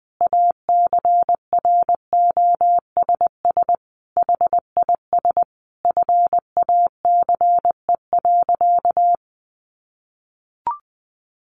Morse code, heard as a keyed tone, about 20 wpm, 700 Hz. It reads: ACROSS HIS FACE.
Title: Sentences from Top 700 Words 20wpm